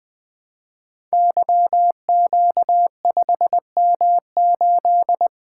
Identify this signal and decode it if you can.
Morse code, heard as a keyed tone, about 20 wpm, 700 Hz. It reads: YQ5M8
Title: Code Groups - 5 Letter:Numbers - 20wpm